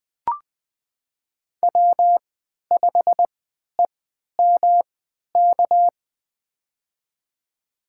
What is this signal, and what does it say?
Morse code, heard as a keyed tone, about 20 wpm, spaced as for 11 wpm, 700 Hz. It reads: W5EMK
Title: USA Call Signs - 5 Characters - 3x Character Spacing - 20wpm